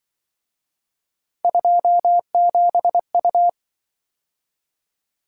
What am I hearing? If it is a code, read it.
Morse code, heard as a keyed tone, about 25 wpm, 700 Hz. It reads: 27U
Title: Speed-Racing - Three Letter-Number Combinations - 20wpm